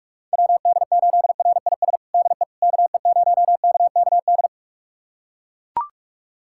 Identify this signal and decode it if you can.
Morse code, heard as a keyed tone, about 42 wpm, 700 Hz. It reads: WD8RIS DE KE0KKD
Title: USA Callsign DE Callsign - 6 Characters - 45wpm